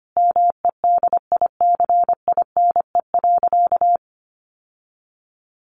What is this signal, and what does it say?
Morse code, heard as a keyed tone, about 25 wpm, 700 Hz. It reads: MEDICINE.
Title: Sentences from Top 900 Words 25wpm